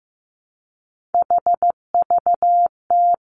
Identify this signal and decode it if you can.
Morse code, heard as a keyed tone, about 15 wpm, 700 Hz. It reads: HVT
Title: Three Challenging Letter-Number Combinations 15wpm